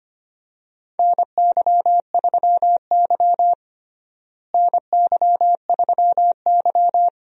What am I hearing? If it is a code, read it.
Morse code, heard as a keyed tone, about 25 wpm, 700 Hz. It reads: NY3Y NY3Y
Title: USA Call Signs - 4 Characters - Repeated 3x - 25wpm